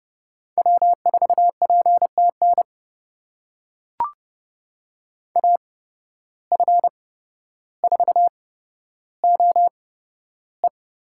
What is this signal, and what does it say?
Morse code, heard as a keyed tone, about 30 wpm, spaced as for 13 wpm, 700 Hz. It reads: W4PTN AF4OE
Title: USA Call Signs - 5 Characters - 8x Character Spacing - 30wpm